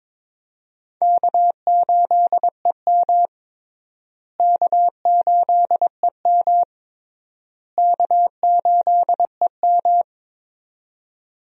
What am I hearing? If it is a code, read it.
Morse code, heard as a keyed tone, about 22 wpm, 700 Hz. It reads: K8EM K8EM K8EM
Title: USA Call Signs - 4 Characters - Repeated 3x - 22wpm